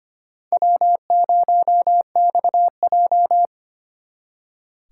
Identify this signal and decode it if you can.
Morse code, heard as a keyed tone, about 32 wpm, 700 Hz. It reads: W0XJ